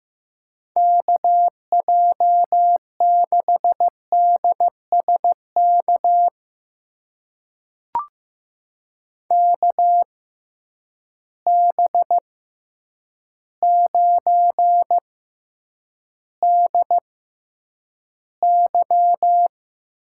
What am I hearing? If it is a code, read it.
Morse code, heard as a keyed tone, about 15 wpm, spaced as for 8 wpm, 700 Hz. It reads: KJ6DSK KB9DY
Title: USA Call Signs - 6 Characters - 6x Character Spacing - 15wpm